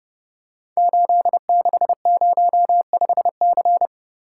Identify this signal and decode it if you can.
Morse code, heard as a keyed tone, about 28 wpm, 700 Hz. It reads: 8605C
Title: Code Groups - 5 Letter:Numbers - 30wpm